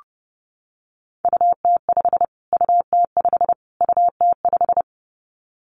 Morse code, heard as a keyed tone, about 28 wpm, 700 Hz. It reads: UT5 UT5 UT5
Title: Callsign Prefixes for Top 25 DXCC - Repeated 3x - 30wpm